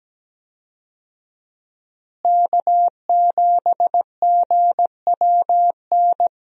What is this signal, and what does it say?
Morse code, heard as a keyed tone, about 17 wpm, 700 Hz. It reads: K7GWN
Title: USA Call Signs - 5 Characters - Repeated 3x - 17wpm